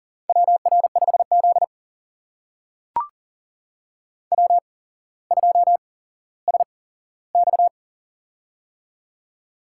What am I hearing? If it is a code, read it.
Morse code, heard as a keyed tone, about 40 wpm, spaced as for 17 wpm, 700 Hz. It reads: WR5Z W2SX
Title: USA Call Signs - 4 Characters - 8x Character Spacing - 40wpm